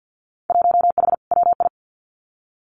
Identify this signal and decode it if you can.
Morse code, heard as a keyed tone, about 50 wpm, 700 Hz. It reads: 1H WI
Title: Winter Field Day Exchanges 50wpm